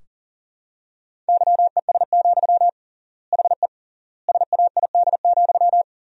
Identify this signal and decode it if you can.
Morse code, heard as a keyed tone, about 40 wpm, 700 Hz. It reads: YES, HE SAID,
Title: 3x Word-Spacing - Sentences from Top 500 Words - 40wpm